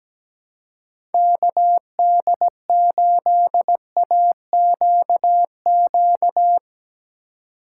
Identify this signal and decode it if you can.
Morse code, heard as a keyed tone, about 17 wpm, 700 Hz. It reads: KD8AQQ
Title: USA Call Signs - 6 Characters - 2x Character Spacing - 17wpm